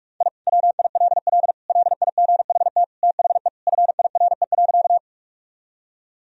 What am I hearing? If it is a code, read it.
Morse code, heard as a keyed tone, about 46 wpm, 700 Hz. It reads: I WILL LIGHT THE FIRE.
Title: Sentences from Top 1000 Words - Review Part 1 - 45wpm